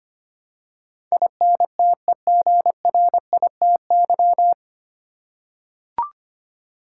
Morse code, heard as a keyed tone, about 25 wpm, 700 Hz. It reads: INTEGRITY
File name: Nine Letter Words - 25wpm